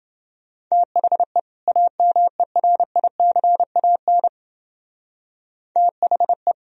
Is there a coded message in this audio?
Yes: Morse code from 0.7 to 6.5 s, heard as a keyed tone, about 28 wpm, 700 Hz. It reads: THE AMERICAN THE